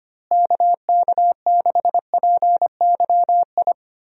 Morse code, heard as a keyed tone, about 25 wpm, 700 Hz. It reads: KK6PYI
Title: USA Call Signs - 6 Characters - Repeated 3x - 25wpm